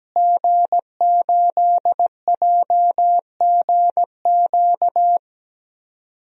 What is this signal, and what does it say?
Morse code, heard as a keyed tone, about 17 wpm, 700 Hz. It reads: G8JGQ
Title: DXCC-100 Call Signs - 5 Characters - Repeated 2x - 17wpm